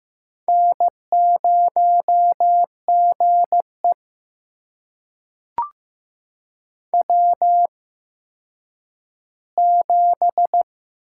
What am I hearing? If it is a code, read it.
Morse code, heard as a keyed tone, about 15 wpm, spaced as for 9 wpm, 700 Hz. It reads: N0GE W7
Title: USA Call Signs - 8x Character Spacing - 15wpm